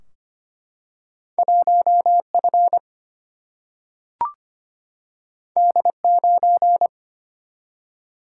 Morse code, heard as a keyed tone, about 25 wpm, 700 Hz. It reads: 1F D9
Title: Single-Letter Single-Number 25wpm